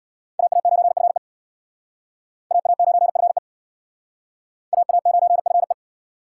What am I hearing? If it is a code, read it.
Morse code, heard as a keyed tone, about 61 wpm, 700 Hz. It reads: AA9FE AA9FE AA9FE
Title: Speed Racing - USA 2x2 Call Signs 50wpm